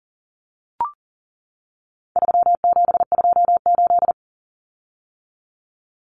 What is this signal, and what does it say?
Morse code, heard as a keyed tone, about 40 wpm, 700 Hz. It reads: 3728
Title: Common Cut Numbers - 4-Digit 40wpm